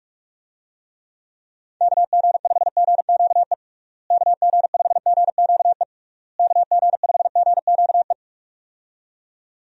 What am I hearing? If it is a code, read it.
Morse code, heard as a keyed tone, about 47 wpm, 700 Hz. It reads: KG5GQE KG5GQE KG5GQE
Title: USA Call Signs - 6 Characters - Repeated 3x - 45wpm